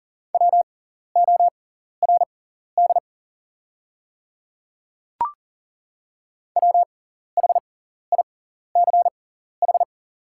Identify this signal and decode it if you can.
Morse code, heard as a keyed tone, about 40 wpm, spaced as for 14 wpm, 700 Hz. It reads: WORD WHICH
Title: 6x Character Spacing - Top 100 Words - 40wpm